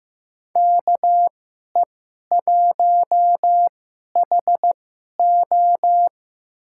Morse code, heard as a keyed tone, about 15 wpm, spaced as for 11 wpm, 700 Hz. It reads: KE1HO
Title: USA Call Signs - 5 Characters - 2x Character Spacing - 15wpm